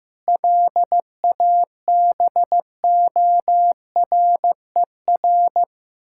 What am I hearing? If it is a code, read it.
Morse code, heard as a keyed tone, about 15 wpm, 700 Hz. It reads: LABORER
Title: Practice QSO - Exchange 3 - More Info - 15wpm